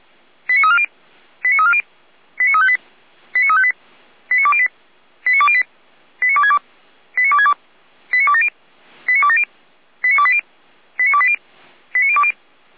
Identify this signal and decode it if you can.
Five-tone selective call (ZVEI), heard as quick runs of five tones, tones of about 70 ms. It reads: 89340 89340 89358 89358 89208 89208 89262 89262 89270 89270 89280 89280 89020
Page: SDR/ZVEI